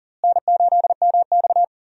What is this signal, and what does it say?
Morse code, heard as a keyed tone, about 40 wpm, 700 Hz. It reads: N8MX
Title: USA Call Signs - 4 Characters - Repeated 2x - 40wpm